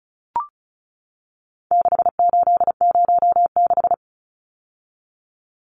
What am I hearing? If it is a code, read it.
Morse code, heard as a keyed tone, about 34 wpm, 700 Hz. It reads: 6806
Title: Numbers - 4-Digit 35wpm